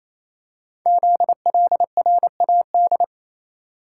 Morse code, heard as a keyed tone, about 29 wpm, 700 Hz. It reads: ZLRAD
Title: Code Groups - 5 Letter:Numbers - 28wpm